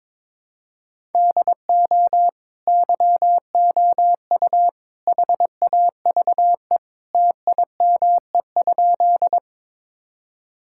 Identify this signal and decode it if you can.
Morse code, heard as a keyed tone, about 22 wpm, 700 Hz. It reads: DO YOU HAVE TIME?
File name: practice / 22wpm / Sentences with 3-Word Phrases - 22wpm